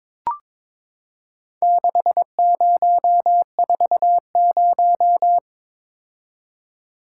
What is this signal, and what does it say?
Morse code, heard as a keyed tone, about 23 wpm, 700 Hz. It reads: 6040
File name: practice / 22wpm / Four Letter-Number Combinations - 22wpm